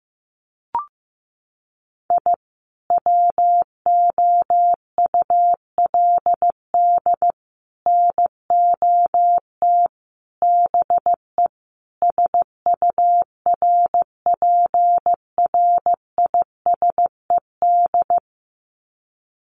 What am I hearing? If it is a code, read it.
Morse code, heard as a keyed tone, about 15 wpm, 700 Hz. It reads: I WOULD NOT BE SURPRISED
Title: Sets of 5 Words Encore2 15wpm